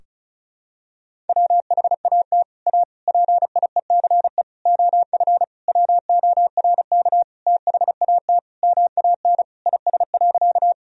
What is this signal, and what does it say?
Morse code, heard as a keyed tone, about 36 wpm, 700 Hz. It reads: WHAT A PIECE OF WORK THAT MAN IS.
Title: Sentences from Top 1000 Words - Review Part 2 - 35wpm